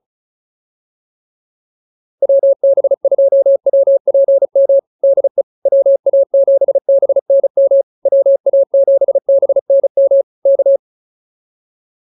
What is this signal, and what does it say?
Morse code, heard as a keyed tone, about 34 wpm, 550 Hz. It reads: WB2WPM DE WA7BNM WA7BNM K